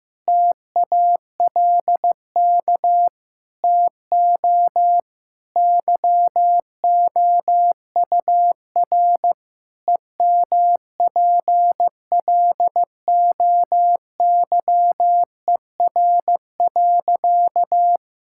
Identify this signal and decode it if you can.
Morse code, heard as a keyed tone, about 16 wpm, 700 Hz. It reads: TALK TO YOUR EMPLOYER.
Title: Sentences from Top 1600 Words 15wpm